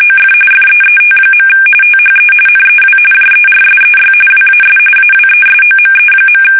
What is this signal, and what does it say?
RTTY, about 45.45 baud. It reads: OF FEET. THE NEW JERSEY WAS AWESOME.
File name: fsk850 / battleshipnj